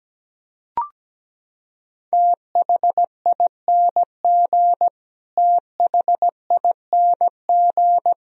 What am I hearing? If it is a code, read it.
Morse code, heard as a keyed tone, about 17 wpm, 700 Hz. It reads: THING THING
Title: Top 100 Words Repeated 17wpm